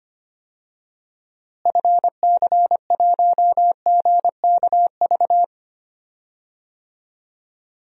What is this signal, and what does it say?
Morse code, heard as a keyed tone, about 25 wpm, 700 Hz. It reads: FC1GKV
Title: DXCC-100 Call Signs - 6 Characters - Repeated 2x - 25wpm